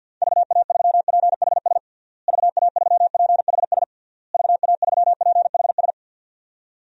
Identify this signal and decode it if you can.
Morse code, heard as a keyed tone, about 50 wpm, 700 Hz. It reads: VA3PHS VA3PHS VA3PHS
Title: DXCC-100 Call Signs - 6 Characters - Repeated 3x - 50wpm